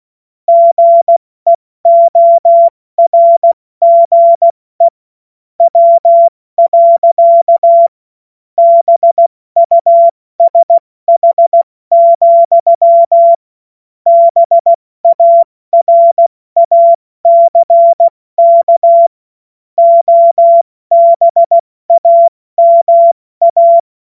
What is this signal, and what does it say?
Morse code, heard as a keyed tone, about 16 wpm, 680 Hz. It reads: GEORGE W. BUSH, BARACK OBAMA
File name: Morsekurs / Witze / Witz 09. Februar 2021, 70 F80